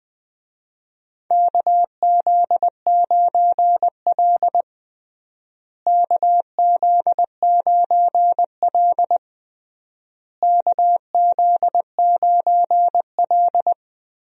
Morse code, heard as a keyed tone, about 20 wpm, 700 Hz. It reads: KZ9L KZ9L KZ9L
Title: USA Call Signs - 4 Characters - Repeated 3x - 20wpm